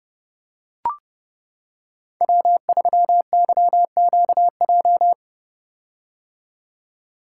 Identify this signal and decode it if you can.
Morse code, heard as a keyed tone, about 30 wpm, 700 Hz. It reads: W3YQJ
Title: Code Groups - 5 Letter:Numbers - 30wpm